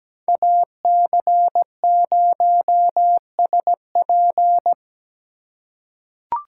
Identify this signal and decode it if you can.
Morse code, heard as a keyed tone, about 17 wpm, 700 Hz. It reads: AC0SP
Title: USA Call Signs - 5 Characters - 3x Character Spacing - 17wpm